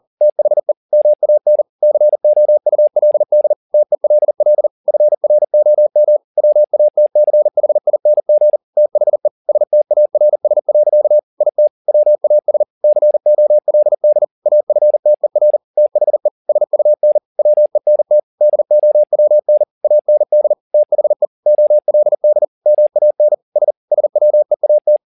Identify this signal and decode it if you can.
Morse code, heard as a keyed tone, about 40 wpm, 600 Hz. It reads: THE MAN COULD TELL FROM WATCHING THE STARS. IT WAS COLD AFTER THE SUN WENT DOWN AND THE OLD MAN S SWEAT